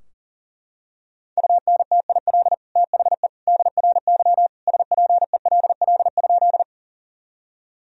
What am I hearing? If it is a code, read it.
Morse code, heard as a keyed tone, about 40 wpm, 700 Hz. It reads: UNTIL THE DRY SPELL?
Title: Sentences from Top 1000 Words - Review Part 3 - 40wpm